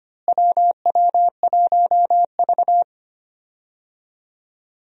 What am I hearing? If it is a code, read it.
Morse code, heard as a keyed tone, about 25 wpm, 700 Hz. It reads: WW1V
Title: USA Call Signs - 4 Characters - Repeated 2x - 25wpm